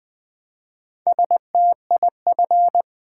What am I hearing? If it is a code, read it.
Morse code, heard as a keyed tone, about 20 wpm, 700 Hz. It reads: STIF
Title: Build A Word - 1 Letter At A Time - 5-Letter Words - 20wpm